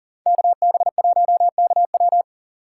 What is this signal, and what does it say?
Morse code, heard as a keyed tone, about 48 wpm, 690 Hz. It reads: KB1KW
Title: Speed Racing - USA Call Signs 40wpm